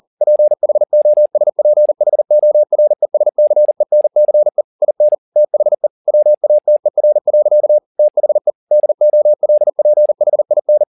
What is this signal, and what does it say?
Morse code, heard as a keyed tone, about 40 wpm, 600 Hz. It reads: PHOSPHORESCENCE IN THE WATER. THE DOLPHIN
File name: the old man and the sea 2 40wpm